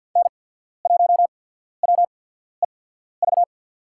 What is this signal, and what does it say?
Morse code, heard as a keyed tone, about 53 wpm, spaced as for 14 wpm, 700 Hz. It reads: N1WEV
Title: USA Call Signs - 5 Characters - 8x Character Spacing - 50wpm